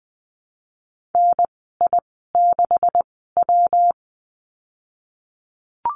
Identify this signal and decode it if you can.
Morse code, heard as a keyed tone, about 20 wpm, spaced as for 15 wpm, 700 Hz. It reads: NI6W
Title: 231 Lesson 37g - Letter x - Call Signs 20wpm